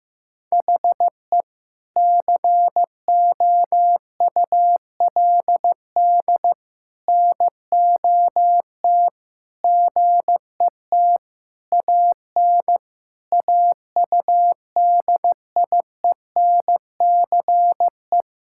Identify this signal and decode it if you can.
Morse code, heard as a keyed tone, about 15 wpm, 700 Hz. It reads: HE COULD NOT GET AN AUDIENCE
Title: Sentences from Top 1000 Words - Review Part 5 - 15wpm